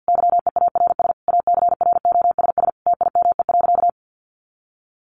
Morse code, heard as a keyed tone, about 50 wpm, 700 Hz. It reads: YEARS ACROSS TIME.
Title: A Sound Of Thunder 50wpm